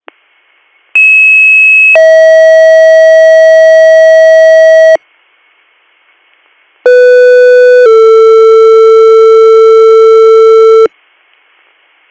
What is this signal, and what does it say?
Two-tone page, heard as a one-second tone, then a three-second tone, about 2600 then 640 Hz, about 500 then 450 Hz. It is heard from 1.0 to 10.9 s.